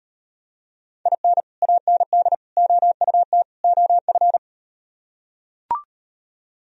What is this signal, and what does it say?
Morse code, heard as a keyed tone, about 38 wpm, 700 Hz. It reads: IN AND OUT OF